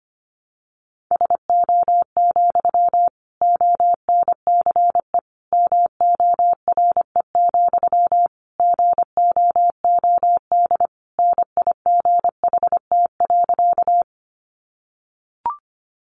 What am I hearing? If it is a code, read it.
Morse code, heard as a keyed tone, about 25 wpm, 700 Hz. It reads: SO, ONCE MORE, GOOD NIGHT.
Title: Sentences from Top 300 Words 25wpm